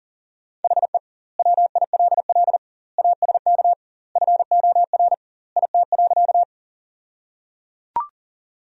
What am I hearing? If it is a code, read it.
Morse code, heard as a keyed tone, about 37 wpm, 700 Hz. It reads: HE WILL ASK FOR IT.
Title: 2x Word-Spacing - Sentences from Top 200 Words - 40wpm